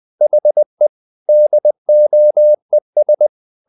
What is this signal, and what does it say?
Morse code, heard as a keyed tone, about 20 wpm, 600 Hz. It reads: HE DOES
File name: the old man and the sea part 2 20wpm